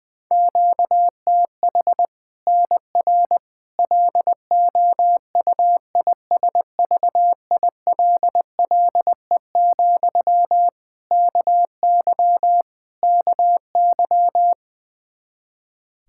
Morse code, heard as a keyed tone, about 20 wpm, 700 Hz. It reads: QTH NR LOUISVILLE, KY KY
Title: K1USN SST QSO Practice 20wpm